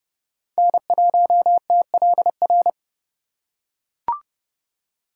Morse code, heard as a keyed tone, about 28 wpm, 700 Hz. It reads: N1TLR